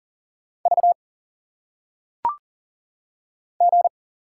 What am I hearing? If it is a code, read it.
Morse code, heard as a keyed tone, about 39 wpm, 700 Hz. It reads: V G
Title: Single Letters - Phonetic Pronunciation 40wpm